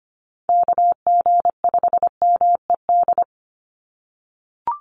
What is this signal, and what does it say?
Morse code, heard as a keyed tone, about 25 wpm, 700 Hz. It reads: KG5MED
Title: USA Call Signs 25wpm